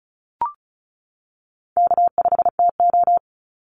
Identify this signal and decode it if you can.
Morse code, heard as a keyed tone, about 34 wpm, 700 Hz. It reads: K5TO
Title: Call Signs - 35wpm